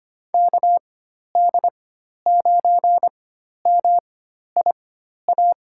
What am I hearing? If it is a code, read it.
Morse code, heard as a keyed tone, about 25 wpm, spaced as for 12 wpm, 700 Hz. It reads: KD9MIA
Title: USA Call Signs - 4x Character Spacing - 25wpm